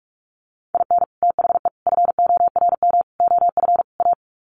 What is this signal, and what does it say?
Morse code, heard as a keyed tone, about 44 wpm, 700 Hz. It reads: IN THE FORM OF A
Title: Sets of 5 Words 45wpm